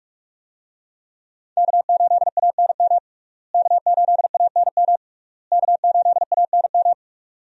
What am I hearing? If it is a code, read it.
Morse code, heard as a keyed tone, about 46 wpm, 700 Hz. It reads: K8ANM K8ANM K8ANM
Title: USA Call Signs - 5 Characters - Repeated 3x - 45wpm